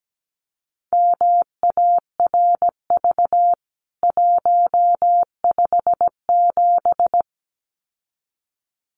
Morse code, heard as a keyed tone, about 17 wpm, 700 Hz. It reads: MARV 157
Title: CWT Exchange 17wpm